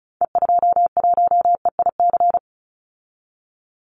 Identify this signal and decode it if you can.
Morse code, heard as a keyed tone, about 34 wpm, 700 Hz. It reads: E21EIC
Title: Call Signs - 35wpm